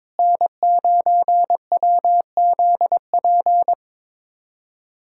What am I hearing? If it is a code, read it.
Morse code, heard as a keyed tone, about 22 wpm, 700 Hz. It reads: N9WZP
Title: USA Callsign DE Callsign - 5 Characters - 22wpm